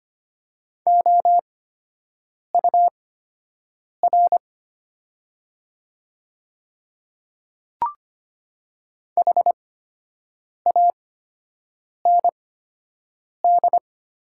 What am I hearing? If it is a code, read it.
Morse code, heard as a keyed tone, about 25 wpm, spaced as for 7 wpm, 700 Hz. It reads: OUR HAND
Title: 8x Character Spacing - Top 500 Words - 25wpm